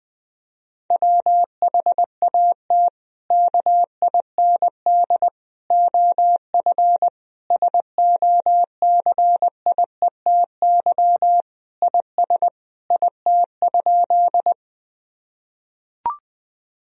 Morse code, heard as a keyed tone, about 20 wpm, 700 Hz. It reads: WHAT KIND OF SOCIETY IS IT?
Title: Sentences from Top 1000 Words - Review Part 4 - 20wpm